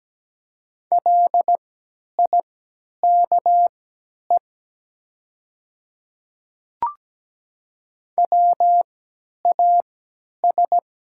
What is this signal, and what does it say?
Morse code, heard as a keyed tone, about 17 wpm, spaced as for 10 wpm, 690 Hz. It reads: LIKE WAS
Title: 3x Character Spacing - Top 100 Words - 17wpm